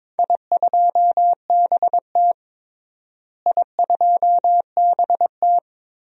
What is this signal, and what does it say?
Morse code, heard as a keyed tone, about 22 wpm, 700 Hz. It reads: I2BT I2BT
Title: DXCC-100 Call Signs - 4 Characters - Repeated 2x - 22wpm